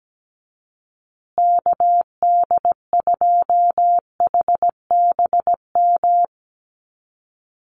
Morse code, heard as a keyed tone, about 17 wpm, 700 Hz. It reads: KD2HBM
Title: USA Call Signs Repeated 3x 17wpm